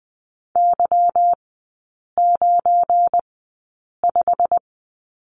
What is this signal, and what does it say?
Morse code, heard as a keyed tone, about 20 wpm, spaced as for 8 wpm, 700 Hz. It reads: Y95
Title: 172 Lesson 28e - Number 9 - Three Characters 20wpm